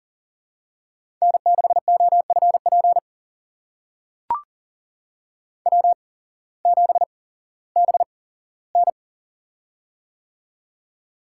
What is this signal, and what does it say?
Morse code, heard as a keyed tone, about 40 wpm, spaced as for 19 wpm, 700 Hz. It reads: N6OFP W7BN
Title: USA Call Signs - 8x Character Spacing - 40wpm